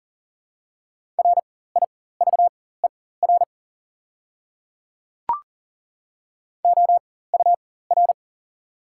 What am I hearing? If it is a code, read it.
Morse code, heard as a keyed tone, about 37 wpm, spaced as for 18 wpm, 700 Hz. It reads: RIVER OUR
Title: 4x Character Spacing - Top 300 Words - 40wpm